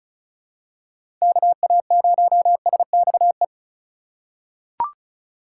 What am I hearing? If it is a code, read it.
Morse code, heard as a keyed tone, about 36 wpm, 700 Hz. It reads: KA0SXE
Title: USA Call Signs - 6 Characters - 3x Character Spacing - 35wpm